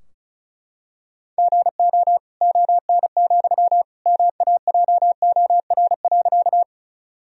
Morse code, heard as a keyed tone, about 36 wpm, 700 Hz. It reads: GO ON, MAJOR.
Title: Sentences from Top 1000 Words - Review Part 4 - 35wpm